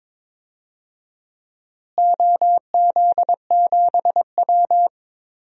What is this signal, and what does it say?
Morse code, heard as a keyed tone, about 21 wpm, 700 Hz. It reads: OZ7W
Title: DXCC-100 Call Signs - 4 Characters - Repeated 2x - 22wpm